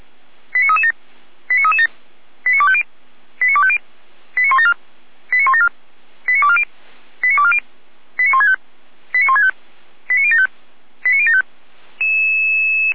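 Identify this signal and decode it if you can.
Five-tone selective call (ZVEI), heard as quick runs of five tones, tones of about 70 ms. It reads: 89337 89337 89250 89250 89174 89174 89240 89240 89165 89165 89075 89075
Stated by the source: Five-tone ZVEI selective call 89336, received by SDR.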